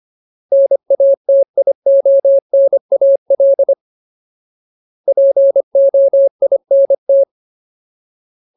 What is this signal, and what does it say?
Morse code, heard as a keyed tone, about 25 wpm, 550 Hz. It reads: NATIONAL POINT
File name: Ultimate Word List / 25WPM Ultimate